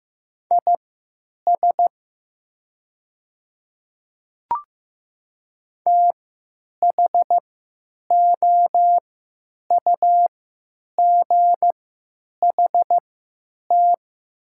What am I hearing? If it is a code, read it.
Morse code, heard as a keyed tone, about 15 wpm, spaced as for 9 wpm, 700 Hz. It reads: IS THOUGHT
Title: 3x Character Spacing - Top 500 Words - 15wpm